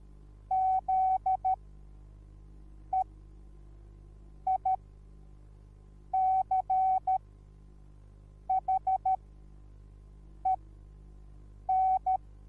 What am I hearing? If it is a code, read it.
Morse code, heard as a keyed tone, about 13 wpm, spaced as for 5 wpm, 760 Hz. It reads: ZEICHEN
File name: Onlinelehrgang / morsen / DJ4UF Morsen21